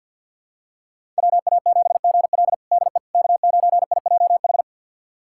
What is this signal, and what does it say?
Morse code, heard as a keyed tone, about 52 wpm, 700 Hz. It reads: WA7GL DE K9IJH
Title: USA Callsign DE Callsign - 5 Characters - 50wpm